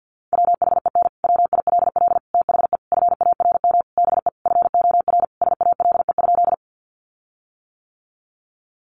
Morse code, heard as a keyed tone, about 48 wpm, 700 Hz. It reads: WHEN WILL THE FARM BE FOR SALE?
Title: Sentences from Top 1400 Words 50wpm